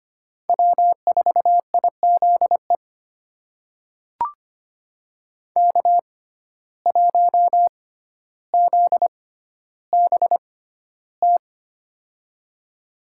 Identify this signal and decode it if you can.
Morse code, heard as a keyed tone, about 25 wpm, spaced as for 13 wpm, 700 Hz. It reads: W4IZE K1ZBT
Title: USA Call Signs - 5 Characters - 6x Character Spacing - 25wpm